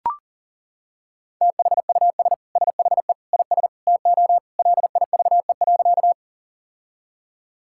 Morse code, heard as a keyed tone, about 40 wpm, 700 Hz. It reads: THUS SHE IS TO LIVE.
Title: Sentences from Top 1000 Words - Review Part 5 - 40wpm